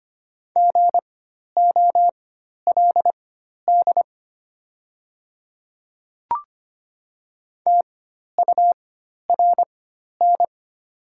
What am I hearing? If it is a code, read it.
Morse code, heard as a keyed tone, about 25 wpm, spaced as for 12 wpm, 700 Hz. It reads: GOLD TURN
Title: 4x Character Spacing - Top 500 Words - 25wpm